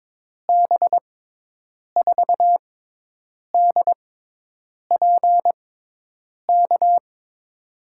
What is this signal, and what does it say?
Morse code, heard as a keyed tone, about 22 wpm, spaced as for 8 wpm, 700 Hz. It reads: B4DPK
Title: USA Call Signs - 6 Characters - 6x Character Spacing - 22wpm